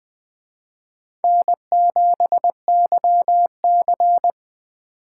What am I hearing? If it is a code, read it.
Morse code, heard as a keyed tone, about 20 wpm, 700 Hz. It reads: N7YC
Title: Speed Racing - USA Call Signs 17wpm